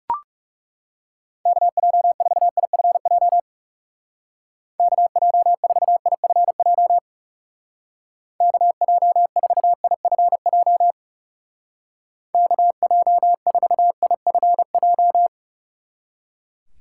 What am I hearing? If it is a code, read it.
Morse code, heard as a keyed tone, about 38 wpm, 700 Hz. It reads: KJ4IFJ KJ4IFJ KJ4IFJ KJ4IFJ